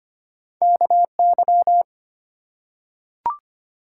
Morse code, heard as a keyed tone, about 25 wpm, 700 Hz. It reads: KY
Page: morse-code-ninja-practice-25wpm